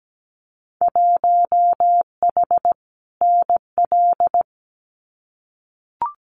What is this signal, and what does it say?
Morse code, heard as a keyed tone, about 17 wpm, 700 Hz. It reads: 1H NL
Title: Winter Field Day Exchanges 17wpm